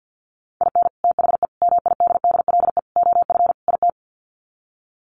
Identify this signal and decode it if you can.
Morse code, heard as a keyed tone, about 50 wpm, 700 Hz. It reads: IN THE MIDDLE OF IT
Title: Sets of 5 Words Encore 50wpm